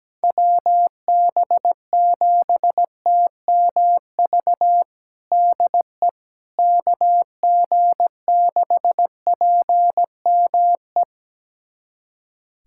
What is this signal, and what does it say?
Morse code, heard as a keyed tone, about 17 wpm, 700 Hz. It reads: WB7TMV DE KG6PME
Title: USA Callsign DE Callsign 17wpm